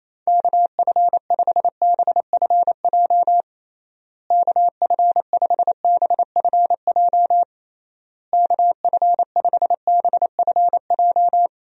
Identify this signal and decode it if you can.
Morse code, heard as a keyed tone, about 28 wpm, 700 Hz. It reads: KF5BFJ KF5BFJ KF5BFJ
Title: USA Call Signs - 6 Characters - Repeated 3x - 28wpm